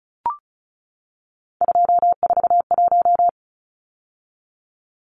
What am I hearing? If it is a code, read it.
Morse code, heard as a keyed tone, about 34 wpm, 700 Hz. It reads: 241
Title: Numbers - 3-Digit 35wpm